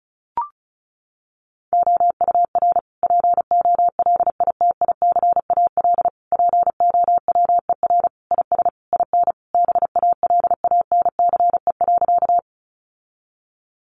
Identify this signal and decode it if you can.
Morse code, heard as a keyed tone, about 34 wpm, 700 Hz. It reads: OUR POLITICAL POWER IS IN BALANCE.
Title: Sentences from Top 900 Words 35wpm